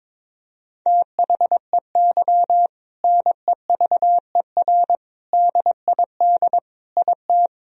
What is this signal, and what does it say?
Morse code, heard as a keyed tone, about 22 wpm, 700 Hz. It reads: THEY NEVER DID IT
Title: Sentences from Top 300 Words and Abbreviations 22wpm